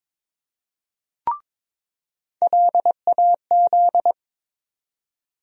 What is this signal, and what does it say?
Morse code, heard as a keyed tone, about 22 wpm, 700 Hz. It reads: LAZ
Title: USA Call Sign Postfixes - 3 Letters 22wpm